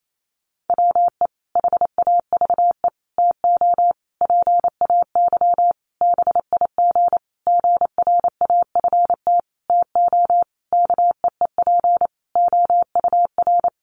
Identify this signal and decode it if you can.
Morse code, heard as a keyed tone, about 28 wpm, 700 Hz. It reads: WE HAVE TO PAY BIG GRAFT TO KEEP OUR
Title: A Sound Of Thunder 28wpm